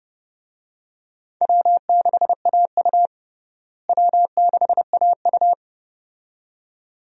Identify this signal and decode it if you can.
Morse code, heard as a keyed tone, about 30 wpm, 700 Hz. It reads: W6AU W6AU
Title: USA Call Signs - 4 Characters - Repeated 2x - 30wpm